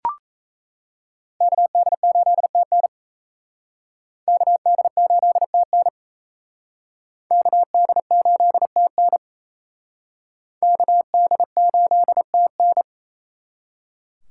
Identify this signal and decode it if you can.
Morse code, heard as a keyed tone, about 38 wpm, 700 Hz. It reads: KD8TN KD8TN KD8TN KD8TN